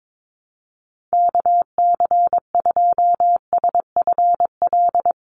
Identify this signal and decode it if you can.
Morse code, heard as a keyed tone, about 22 wpm, 700 Hz. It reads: KC2SFL
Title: USA Call Signs Repeated 2x 22wpm